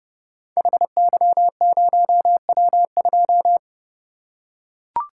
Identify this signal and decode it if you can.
Morse code, heard as a keyed tone, about 30 wpm, 700 Hz. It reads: HY0W2
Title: Code Groups - 5 Letter:Numbers - 30wpm